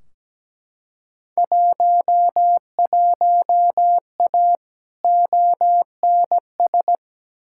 Morse code, heard as a keyed tone, about 17 wpm, 700 Hz. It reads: 11A ONS
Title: Field Day Exchanges Encore - 17wpm